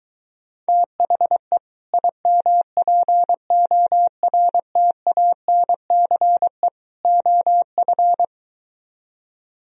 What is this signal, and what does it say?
Morse code, heard as a keyed tone, about 23 wpm, 700 Hz. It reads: THE IMPORTANCE OF